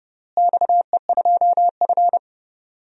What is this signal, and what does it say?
Morse code, heard as a keyed tone, about 30 wpm, 700 Hz. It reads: XE2F
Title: DXCC-100 Call Signs - 4 Characters - Repeated 3x - 30wpm